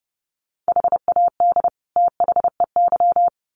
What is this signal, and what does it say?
Morse code, heard as a keyed tone, about 30 wpm, 700 Hz. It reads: HAD THEY
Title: Sentences from Top 400 Words 30wpm